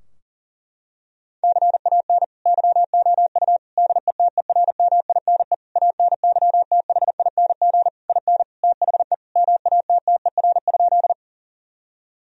Morse code, heard as a keyed tone, about 40 wpm, 700 Hz. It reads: CAN YOU DETERMINE ANYTHING IN THE MATTER?
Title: Sentences from Top 1000 Words - Review Part 4 - 40wpm